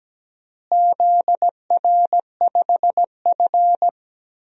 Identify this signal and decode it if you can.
Morse code, heard as a keyed tone, about 17 wpm, 700 Hz. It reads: ZR5F
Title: DXCC-100 Call Signs - 4 Characters - Repeated 3x - 17wpm